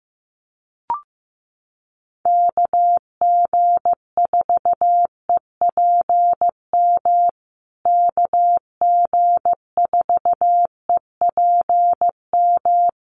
Morse code, heard as a keyed tone, about 15 wpm, 700 Hz. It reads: KG4EPM KG4EPM
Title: USA Call Signs Repeated 2x 15wpm